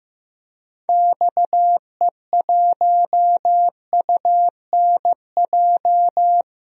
Morse code, heard as a keyed tone, about 15 wpm, 700 Hz. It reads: XE1UNJ
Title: DXCC-100 Call Signs - 6 Characters - 15wpm